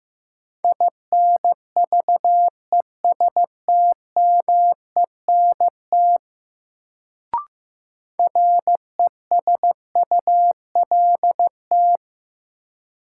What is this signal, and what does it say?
Morse code, heard as a keyed tone, about 15 wpm, 700 Hz. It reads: INVESTMENT RESULT
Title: Sentences from Top 700 Words - New Word Drill - 15wpm